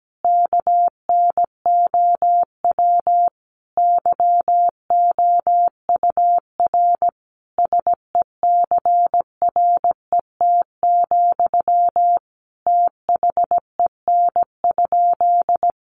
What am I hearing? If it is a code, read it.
Morse code, heard as a keyed tone, about 17 wpm, 700 Hz. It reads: KNOW YOUR SECRET, THEN?
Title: Sentences from Top 1200 Words 17wpm